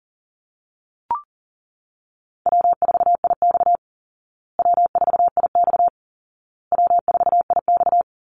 Morse code, heard as a keyed tone, about 40 wpm, 700 Hz. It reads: W4IX W4IX W4IX
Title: Easy Call Signs Repeated 3x - 40wpm